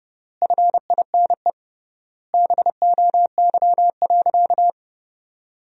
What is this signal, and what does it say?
Morse code, heard as a keyed tone, about 32 wpm, 700 Hz. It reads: FINE BOY.
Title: 3x Word-Spacing - Sentences from Top 500 Words - 30wpm